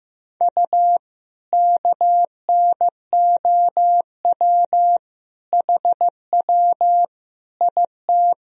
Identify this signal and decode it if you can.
Morse code, heard as a keyed tone, about 15 wpm, 700 Hz. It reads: U KNOW HW IT
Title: Sentences from Top 100 Words and Abbreviations 15wpm